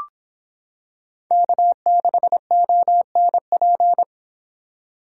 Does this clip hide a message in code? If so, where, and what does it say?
Yes: Morse code from 1.3 to 4.0 s, heard as a keyed tone, about 26 wpm, 700 Hz. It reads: K6ONP